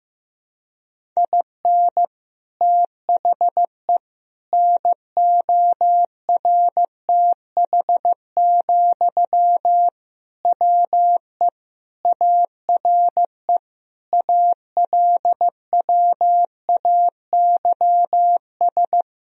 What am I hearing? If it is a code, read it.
Morse code, heard as a keyed tone, about 15 wpm, 700 Hz. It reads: IN THE NORTH, WE ARE ALWAYS
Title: Sentences from Top 1000 Words - Review Part 1 - 15wpm